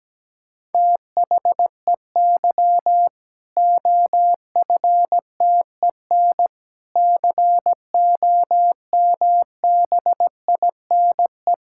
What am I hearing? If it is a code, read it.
Morse code, heard as a keyed tone, about 17 wpm, 700 Hz. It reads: THEY OFTEN COMBINE
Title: Sentences from Top 1600 Words 17wpm